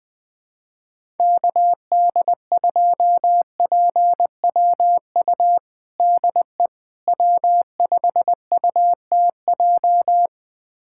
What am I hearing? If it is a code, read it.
Morse code, heard as a keyed tone, about 20 wpm, 700 Hz. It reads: KD2PWU DE W5UTJ
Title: USA Callsign DE Callsign 20wpm